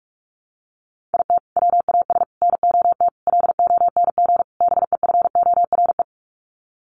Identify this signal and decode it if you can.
Morse code, heard as a keyed tone, about 44 wpm, 700 Hz. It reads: IT WAS NOT LONG BEFORE
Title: Sets of 5 Words Encore2 45wpm